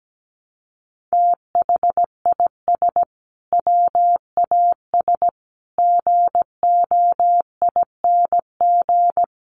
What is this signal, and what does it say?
Morse code, heard as a keyed tone, about 17 wpm, 700 Hz. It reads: THIS WAS GOING
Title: Sets of 5 Words Encore2 17wpm